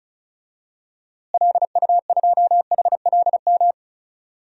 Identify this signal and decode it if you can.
Morse code, heard as a keyed tone, about 34 wpm, 700 Hz. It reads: LU2HLM
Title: DXCC-100 Call Signs - 6 Characters - Repeated 3x - 35wpm